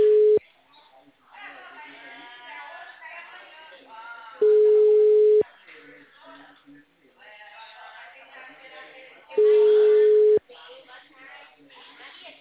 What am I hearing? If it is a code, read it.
Ringback tone (Europe, 425 Hz), with speech.
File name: out- -2009